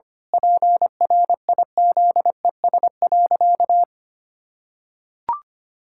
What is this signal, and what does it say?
Morse code, heard as a keyed tone, about 25 wpm, 700 Hz. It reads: PRIZES.
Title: Pangram Challenge v2 - 25wpm